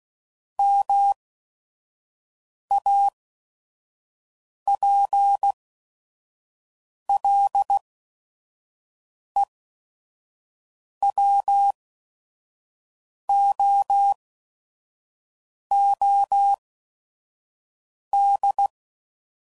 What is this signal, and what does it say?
Morse code, heard as a keyed tone, about 16 wpm, spaced as for 5 wpm, 800 Hz. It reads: MAPLEWOOD
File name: CW Code Groups @5WPM / fromSM's / Cities